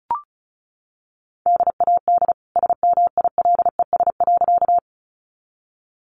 Morse code, heard as a keyed tone, about 34 wpm, 700 Hz. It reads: DAD SMILES.
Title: Easy Phrases 35wpm